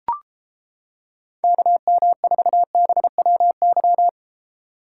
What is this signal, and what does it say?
Morse code, heard as a keyed tone, about 29 wpm, 700 Hz. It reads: KM4BWY